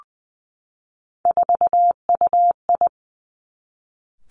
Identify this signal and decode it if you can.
Morse code, heard as a keyed tone, about 19 wpm, 700 Hz. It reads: 4UI
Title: Three Challenging Letter-Number Combinations 20wpm